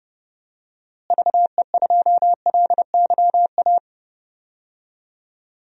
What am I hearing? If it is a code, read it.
Morse code, heard as a keyed tone, about 30 wpm, 700 Hz. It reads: VE2LYA
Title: DXCC-100 Call Signs - 6 Characters - Repeated 2x - 30wpm